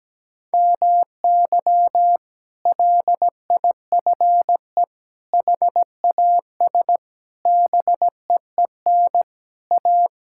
Morse code, heard as a keyed tone, about 17 wpm, 700 Hz. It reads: MY LIFE HAS BEEN A
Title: Sentences from Top 1000 Words - Review Part 6 - 17wpm